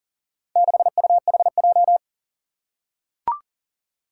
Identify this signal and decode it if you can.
Morse code, heard as a keyed tone, about 40 wpm, 700 Hz. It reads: 6UHJ
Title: Four Challenging Letter-Number Combinations 40wpm